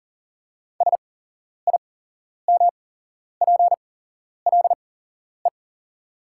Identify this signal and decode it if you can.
Morse code, heard as a keyed tone, about 40 wpm, spaced as for 11 wpm, 700 Hz. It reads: SIMPLE
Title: 8x Character Spacing - Top 500 Words - 40wpm